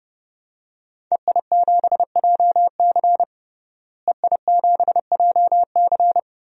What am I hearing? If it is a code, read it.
Morse code, heard as a keyed tone, about 30 wpm, 700 Hz. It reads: EI7JC EI7JC
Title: DXCC-100 Call Signs - 5 Characters - Repeated 2x - 30wpm